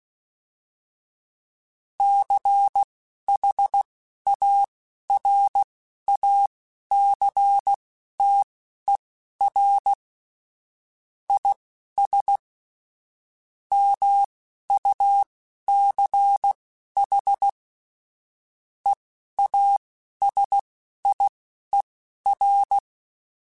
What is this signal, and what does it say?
Morse code, heard as a keyed tone, about 16 wpm, spaced as for 11 wpm, 800 Hz. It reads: CHARACTER IS MUCH EASIER
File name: Quotes / Quotes-mp3s / Character @10